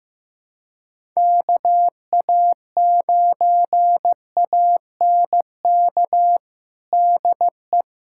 Morse code, heard as a keyed tone, about 16 wpm, 700 Hz. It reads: KA9ANK DE
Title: USA Callsign DE Callsign - 6 Characters - 15wpm